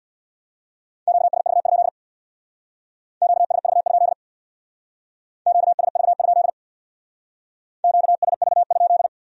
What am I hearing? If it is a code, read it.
Morse code, heard as a keyed tone, about 62 wpm, 700 Hz. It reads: QSV? QSV? QSV? QSV?
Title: Speed-Racing Q-Codes - 50wpm